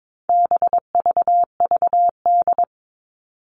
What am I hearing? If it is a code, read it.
Morse code, heard as a keyed tone, about 22 wpm, 700 Hz. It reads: BVVD
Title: All Cut Numbers - 4-Digit 22wpm